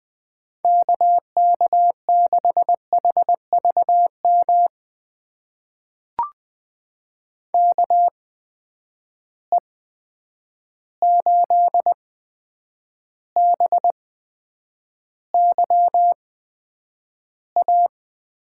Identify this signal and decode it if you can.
Morse code, heard as a keyed tone, about 20 wpm, spaced as for 9 wpm, 700 Hz. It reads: KK6HVM KE8BYA
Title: USA Call Signs - 6 Characters - 8x Character Spacing - 20wpm